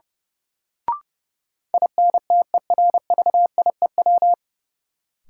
Morse code, heard as a keyed tone, about 31 wpm, 700 Hz. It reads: INTERVIEW
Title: Sentences from Top 1600 Words - New Word Drill Alt - 30wpm